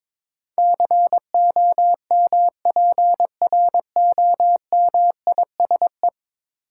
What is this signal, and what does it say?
Morse code, heard as a keyed tone, about 22 wpm, 700 Hz. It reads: COMPROMISE
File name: practice / 22wpm / 10 Letter Words - First 3 Letters Given - 22wpm